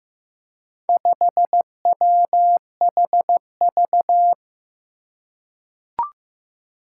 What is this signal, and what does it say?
Morse code, heard as a keyed tone, about 15 wpm, 700 Hz. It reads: 5WHV
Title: Four Challenging Letter-Number Combinations 15wpm